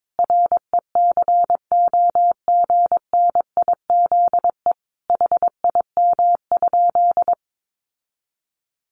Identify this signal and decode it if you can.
Morse code, heard as a keyed tone, about 22 wpm, 700 Hz. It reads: RECOGNIZE HIM?
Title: Sentences from Top 1400 Words 22wpm